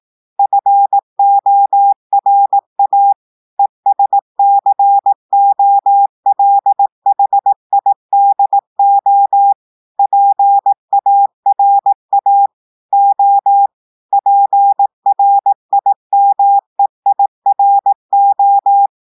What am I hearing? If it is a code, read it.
Morse code, heard as a keyed tone, about 18 wpm, 800 Hz. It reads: FORA ESCOLHIDO PARA O PRIMEIRO
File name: audios / portugais / portugais texte en clair 1 vitesse 18 mots minutes